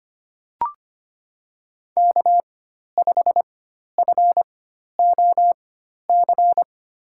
Morse code, heard as a keyed tone, about 25 wpm, spaced as for 12 wpm, 700 Hz. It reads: K5FOC
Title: USA Call Signs - 5 Characters - 4x Character Spacing - 25wpm